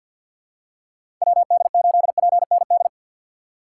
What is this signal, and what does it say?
Morse code, heard as a keyed tone, about 52 wpm, 700 Hz. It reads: WD8PND
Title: USA Call Signs - 6 Characters - 8x Character Spacing - 50wpm